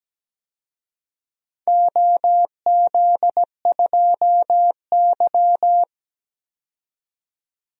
Morse code, heard as a keyed tone, about 17 wpm, 700 Hz. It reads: OZ2Y
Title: DXCC-100 Call Signs - 4 Characters - 17wpm